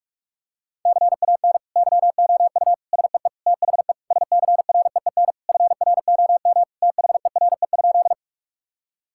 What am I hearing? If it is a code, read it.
Morse code, heard as a keyed tone, about 41 wpm, 700 Hz. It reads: CAN YOU SEE THE SCREEN FROM THERE?
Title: Sentences from Top 1000 Words - Review Part 6 - 45wpm